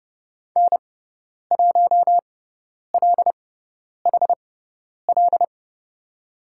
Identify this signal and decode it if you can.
Morse code, heard as a keyed tone, about 30 wpm, spaced as for 10 wpm, 700 Hz. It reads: N1LHL
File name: Lesson 17g - Letter u - Call Signs 30wpm